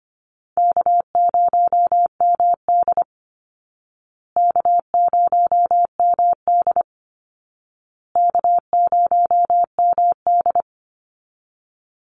Morse code, heard as a keyed tone, about 25 wpm, 700 Hz. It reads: K0MD K0MD K0MD
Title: Easy Call Signs Repeated 3x - 25wpm